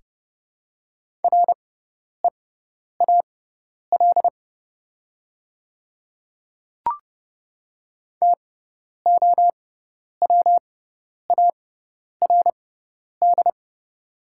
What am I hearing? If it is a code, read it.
Morse code, heard as a keyed tone, about 31 wpm, spaced as for 10 wpm, 700 Hz. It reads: REAL TOWARD
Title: 6x Character Spacing - Top 500 Words - 30wpm